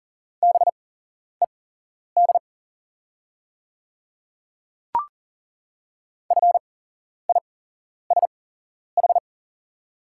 Morse code, heard as a keyed tone, about 40 wpm, spaced as for 11 wpm, 690 Hz. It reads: BED FISH
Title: 8x Character Spacing - Top 500 Words - 40wpm